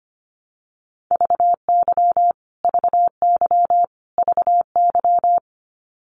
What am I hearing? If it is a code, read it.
Morse code, heard as a keyed tone, about 25 wpm, 700 Hz. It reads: VY VY VY
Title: Abbreviations Repeated 3x - 25wpm